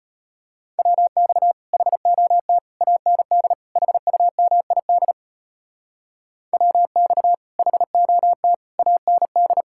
Morse code, heard as a keyed tone, about 32 wpm, 700 Hz. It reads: WX HOT AND HUMID WX HOT AND